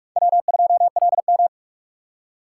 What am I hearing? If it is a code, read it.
Morse code, heard as a keyed tone, about 60 wpm, spaced as for 53 wpm, 700 Hz. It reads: W2LM
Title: Speed Racing - USA 1x2 Call Signs 45wpm